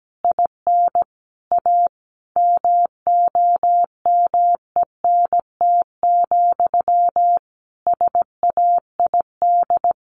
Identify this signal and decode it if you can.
Morse code, heard as a keyed tone, about 17 wpm, 700 Hz. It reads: IN A MOMENT, SAID
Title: Sentences from Top 1000 Words 17wpm